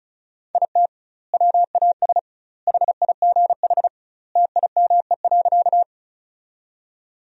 Morse code, heard as a keyed tone, about 36 wpm, 700 Hz. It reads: IT WAS HIGH TIME.
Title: 2x Word-Spacing - Sentences from Top 200 Words - 35wpm